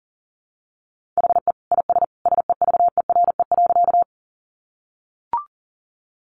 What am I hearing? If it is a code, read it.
Morse code, heard as a keyed tone, about 40 wpm, 700 Hz. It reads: HE IS SEVERE.
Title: Sentences from Top 1400 Words 40wpm